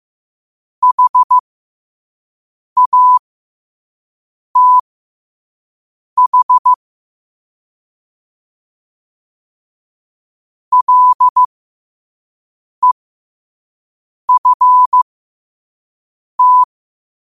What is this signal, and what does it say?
Morse code, heard as a keyed tone, about 15 wpm, spaced as for 5 wpm, 1000 Hz. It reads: HATH LEFT
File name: qotdfiles / 5wpm